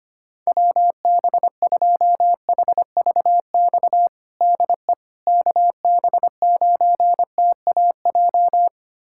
Morse code, heard as a keyed tone, about 25 wpm, 700 Hz. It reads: WB2HVX DE KB9TAJ
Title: USA Callsign DE Callsign - 6 Characters - 25wpm